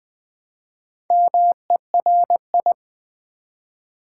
Morse code, heard as a keyed tone, about 20 wpm, 700 Hz. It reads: MERI
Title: Build A Word - 1 Letter At A Time - 5-Letter Words - 20wpm